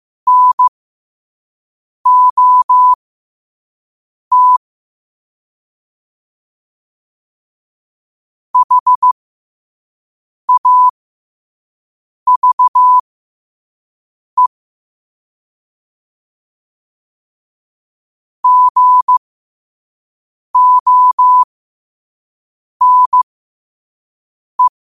Morse code, heard as a keyed tone, about 15 wpm, spaced as for 5 wpm, 1000 Hz. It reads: NOT HAVE GONE
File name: qotdfiles / 5wpm